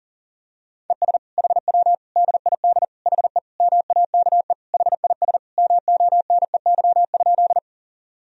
Morse code, heard as a keyed tone, about 40 wpm, 700 Hz. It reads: ES HW DID HE MAKE HIS MONEY?
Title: Sentences from Top 400 Words and Abbreviations 40wpm